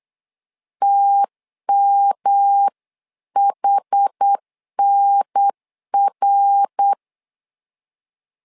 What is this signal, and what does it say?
Morse code, heard as a keyed tone, about 8 wpm, 800 Hz. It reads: TMHNR
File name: LEZIONI / 4 Q1S3 8wpm 3037